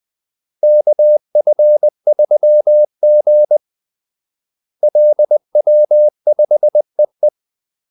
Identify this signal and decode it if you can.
Morse code, heard as a keyed tone, about 20 wpm, 600 Hz. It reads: KF3G LW5EE
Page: Kaldesignaler 20 wpm